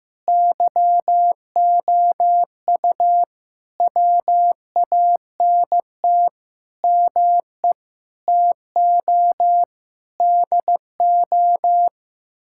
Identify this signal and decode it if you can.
Morse code, heard as a keyed tone, about 16 wpm, 700 Hz. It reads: YOU WANT ME TO DO
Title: Sentences from Top 1000 Words - Review Part 6 - 15wpm